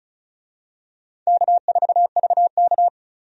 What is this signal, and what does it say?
Morse code, heard as a keyed tone, about 36 wpm, 700 Hz. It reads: K4VK
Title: USA Call Signs - 4 Characters - 35wpm